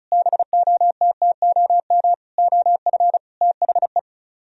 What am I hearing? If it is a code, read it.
Morse code, heard as a keyed tone, about 43 wpm, 700 Hz. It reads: BOTTOM OF THE
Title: Speed-Racing - Sets of 5 Words 35wpm